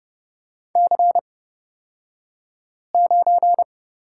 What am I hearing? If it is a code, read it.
Morse code, heard as a keyed tone, about 30 wpm, spaced as for 5 wpm, 700 Hz. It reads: C9
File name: Lesson 38d - Letter q - Two Characters 30wpm